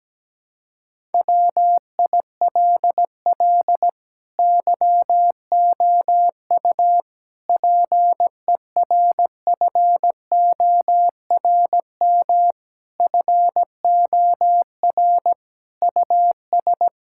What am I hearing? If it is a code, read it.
Morse code, heard as a keyed tone, about 17 wpm, 700 Hz. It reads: WILL YOU PERFORM FOR US
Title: Sentences from Top 1000 Words - Review Part 5 - 17wpm